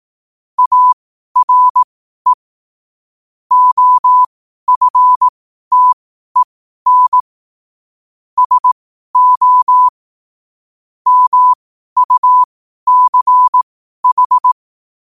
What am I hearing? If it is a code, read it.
Morse code, heard as a keyed tone, about 18 wpm, spaced as for 12 wpm, 1000 Hz. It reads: ARE OFTEN SO MUCH
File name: 12wpm